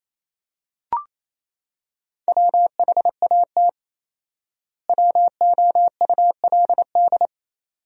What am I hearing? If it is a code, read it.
Morse code, heard as a keyed tone, about 28 wpm, 700 Hz. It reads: WHAT WOULD
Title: 4x Word-Spacing - Sentences from Top 100 Words - 28wpm